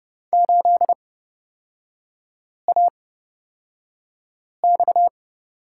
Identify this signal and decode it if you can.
Morse code, heard as a keyed tone, about 31 wpm, spaced as for 5 wpm, 690 Hz. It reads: 8AX
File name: Lesson 37e - Letter x - Three Characters 30wpm